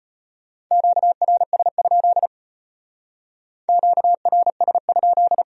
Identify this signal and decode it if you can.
Morse code, heard as a keyed tone, about 32 wpm, 700 Hz. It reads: QRS? QRS?